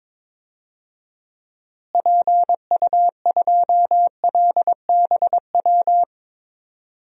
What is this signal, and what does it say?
Morse code, heard as a keyed tone, about 22 wpm, 700 Hz. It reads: PU2LBW
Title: DXCC-100 Call Signs - 6 Characters - Repeated 3x - 22wpm